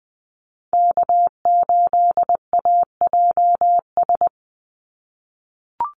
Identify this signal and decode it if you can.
Morse code, heard as a keyed tone, about 20 wpm, 700 Hz. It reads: K8AJS
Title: Call Signs - 20wpm